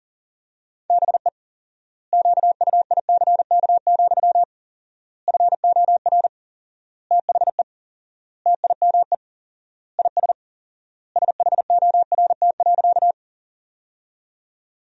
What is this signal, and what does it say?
Morse code, heard as a keyed tone, about 37 wpm, 700 Hz. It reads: BE QUICK, FOR THE TIME IS SHORT.
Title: 4x Word-Spacing - Sentences from Top 500 Words - 40wpm